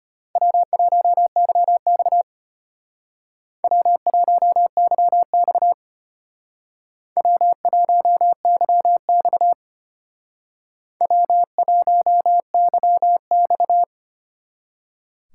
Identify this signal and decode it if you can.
Morse code, heard as a keyed tone, about 33 wpm, 700 Hz. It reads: W1YX W1YX W1YX W1YX